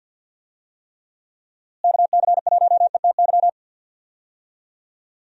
Morse code, heard as a keyed tone, about 52 wpm, 700 Hz. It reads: KC1ETY
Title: USA Call Signs - 6 Characters - Repeated 3x - 50wpm